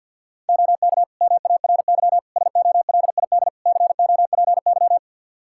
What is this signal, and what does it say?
Morse code, heard as a keyed tone, about 51 wpm, 700 Hz. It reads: OK MARY SOLID COPY
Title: Practice QSO - Exchange 3 - More Info - 50wpm